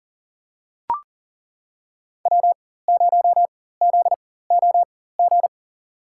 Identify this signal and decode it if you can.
Morse code, heard as a keyed tone, about 39 wpm, spaced as for 19 wpm, 700 Hz. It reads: W0ZOG
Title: USA Call Signs - 5 Characters - 4x Character Spacing - 40wpm